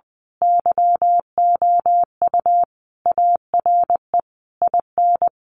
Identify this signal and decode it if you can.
Morse code, heard as a keyed tone, about 19 wpm, 700 Hz. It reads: YOU ARE IN
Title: Sentences from Top 1500 Words 20wpm